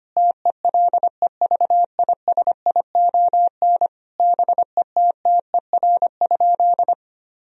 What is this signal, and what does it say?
Morse code, heard as a keyed tone, about 25 wpm, 700 Hz. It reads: TELEVISION BETTER?
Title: Sentences from Top 1000 Words - Review Part 4 - 25wpm